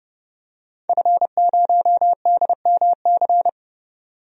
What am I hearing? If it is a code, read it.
Morse code, heard as a keyed tone, about 32 wpm, 700 Hz. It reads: F0DMC
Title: DXCC-100 Call Signs - 5 Characters - 30wpm